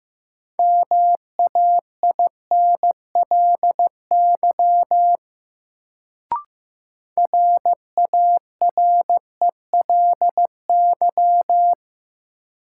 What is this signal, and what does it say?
Morse code, heard as a keyed tone, about 15 wpm, 700 Hz. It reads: MAINLY RARELY
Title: Sentences from Top 1400 Words - New Word Drill Alt - 15wpm